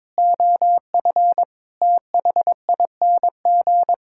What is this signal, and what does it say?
Morse code, heard as a keyed tone, about 22 wpm, 700 Hz. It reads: OF THING
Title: Sentences from Top 1000 Words - Review Part 6 - 22wpm